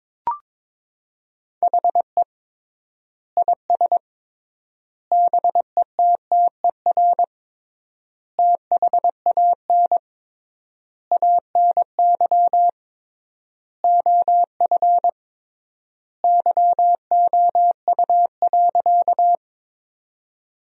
Morse code, heard as a keyed tone, about 22 wpm, 700 Hz. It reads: HE IS BETTER THAN ANY OF YOU.
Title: 3x Word-Spacing - Sentences from Top 400 Words - 22wpm